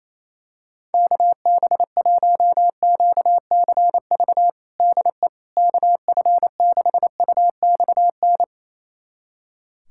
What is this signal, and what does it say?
Morse code, heard as a keyed tone, about 28 wpm, 700 Hz. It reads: KB1QCV DE KF6UXN
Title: USA Callsign DE Callsign - 6 Characters - 28wpm